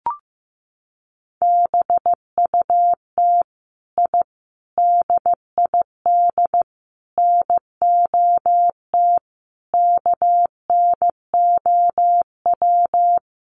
Morse code, heard as a keyed tone, about 15 wpm, 700 Hz. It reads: BUT I DID NOT KNOW
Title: Sets of 5 Words Encore 15wpm